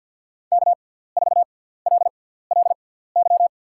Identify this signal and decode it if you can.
Morse code, heard as a keyed tone, about 52 wpm, spaced as for 17 wpm, 700 Hz. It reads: K4LLY
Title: USA Call Signs - 6x Character Spacing - 50wpm